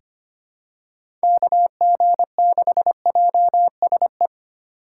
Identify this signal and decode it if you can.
Morse code, heard as a keyed tone, about 25 wpm, 700 Hz. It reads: KG6JSE
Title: USA Call Signs - 6 Characters - 3x Character Spacing - 25wpm